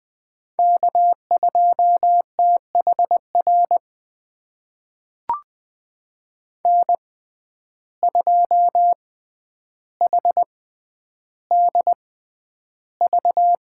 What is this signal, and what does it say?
Morse code, heard as a keyed tone, about 20 wpm, spaced as for 10 wpm, 700 Hz. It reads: K2THR N2HDV
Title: USA Call Signs - 5 Characters - 6x Character Spacing - 20wpm